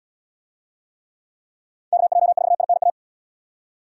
Morse code, heard as a keyed tone, about 75 wpm, 700 Hz. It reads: KJ4INA